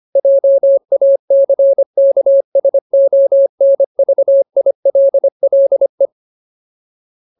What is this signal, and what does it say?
Morse code, heard as a keyed tone, about 25 wpm, 550 Hz. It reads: JACKSONVILLE
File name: Ultimate Word List / 25WPM Ultimate